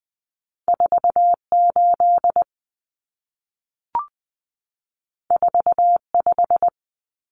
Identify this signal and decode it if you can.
Morse code, heard as a keyed tone, about 20 wpm, 700 Hz. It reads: 48 45
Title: Common Cut Numbers - 2-Digit 20wpm